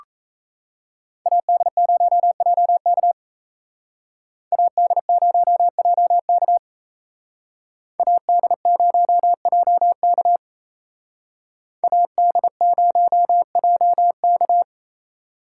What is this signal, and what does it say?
Morse code, heard as a keyed tone, about 34 wpm, 700 Hz. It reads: AD0JK AD0JK AD0JK AD0JK
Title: Speed Racing - USA 2x2 Call Signs 28wpm